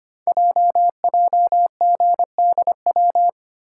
Morse code, heard as a keyed tone, about 25 wpm, 700 Hz. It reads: JJGDW
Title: Code Groups - 5 Letters - 25wpm